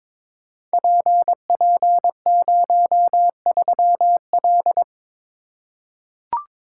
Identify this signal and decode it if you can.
Morse code, heard as a keyed tone, about 22 wpm, 700 Hz. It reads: PP03L
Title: Code Groups - 5 Letter:Numbers - 22wpm